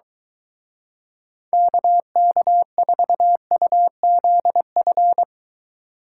Morse code, heard as a keyed tone, about 23 wpm, 700 Hz. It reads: KK4UZF